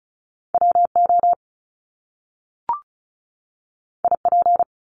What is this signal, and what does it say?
Morse code, heard as a keyed tone, about 36 wpm, 700 Hz. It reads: WO IP
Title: Top Two Letter Combinations in Words 35wpm